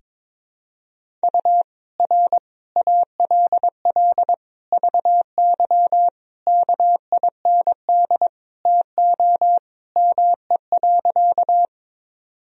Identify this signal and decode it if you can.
Morse code, heard as a keyed tone, about 22 wpm, 700 Hz. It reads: U R ALL VY KIND TO ME.
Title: Sentences from Top 200 Words and Abbreviations 22wpm